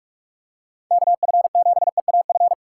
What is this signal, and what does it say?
Morse code, heard as a keyed tone, about 43 wpm, 700 Hz. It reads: KF7EAF
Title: USA Call Signs - 6 Characters - 3x Character Spacing - 45wpm